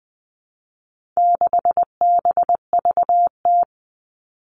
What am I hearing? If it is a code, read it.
Morse code, heard as a keyed tone, about 20 wpm, 700 Hz. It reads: 6BVT
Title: All Cut Numbers - 4-Digit 20wpm